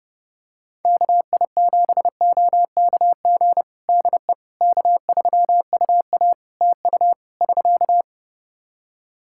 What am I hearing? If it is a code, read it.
Morse code, heard as a keyed tone, about 30 wpm, 700 Hz. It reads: KI7OKG DE K3UA TU <SK>
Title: Practice QSO - Exchange 4 - Ending - 30wpm